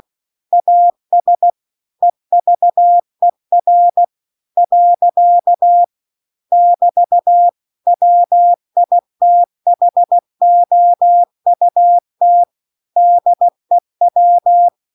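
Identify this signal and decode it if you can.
Morse code, heard as a keyed tone, about 16 wpm, 700 Hz. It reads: AS EVER . <BT> WITHOUT DEW
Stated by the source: Morse practice text at 16 WPM.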